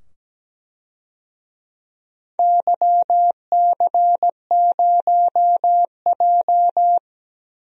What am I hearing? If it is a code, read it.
Morse code, heard as a keyed tone, about 17 wpm, 700 Hz. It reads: YC0J
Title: DXCC-100 Call Signs - 4 Characters - 17wpm